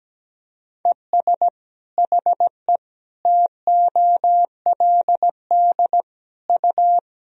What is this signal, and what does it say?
Morse code, heard as a keyed tone, about 17 wpm, 700 Hz. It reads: ES HE TOLD U
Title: Sentences from Top 500 Words and Abbreviations 17wpm